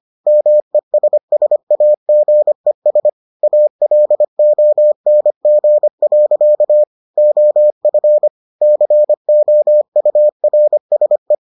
Morse code, heard as a keyed tone, about 25 wpm, 600 Hz. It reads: MESSAGES ALONG. OF COURSE